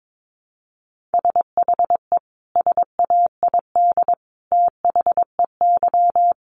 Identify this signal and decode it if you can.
Morse code, heard as a keyed tone, about 22 wpm, 700 Hz. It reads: SHE SAID THEY
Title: Sentences from Top 1100 Words 22wpm